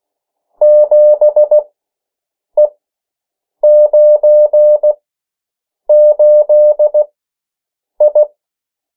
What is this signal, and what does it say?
Morse code, heard as a keyed tone, about 16 wpm, spaced as for 7 wpm, 600 Hz. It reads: 7E98I
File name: morse mp3 / 14 urok 13